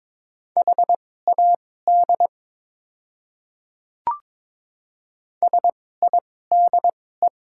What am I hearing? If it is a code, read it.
Morse code, heard as a keyed tone, about 22 wpm, spaced as for 16 wpm, 690 Hz. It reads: HAD SIDE
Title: 2x Character Spacing - Top 100 Words - 22wpm